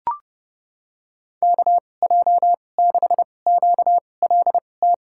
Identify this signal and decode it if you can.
Morse code, heard as a keyed tone, about 32 wpm, spaced as for 22 wpm, 700 Hz. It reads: KJ6QLT
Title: USA Call Signs - 6 Characters - 2x Character Spacing - 30wpm